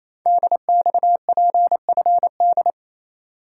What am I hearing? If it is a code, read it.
Morse code, heard as a keyed tone, about 29 wpm, 700 Hz. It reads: DXPFD
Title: Code Groups - 5 Letters - 28wpm